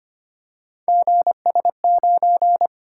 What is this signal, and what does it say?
Morse code, heard as a keyed tone, about 25 wpm, 700 Hz. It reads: GS9
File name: Three Letter-Number Combinations - 25wpm